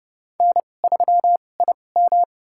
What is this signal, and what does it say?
Morse code, heard as a keyed tone, about 30 wpm, spaced as for 22 wpm, 700 Hz. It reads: N3IM
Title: USA Call Signs - 4 Characters - 2x Character Spacing - 30wpm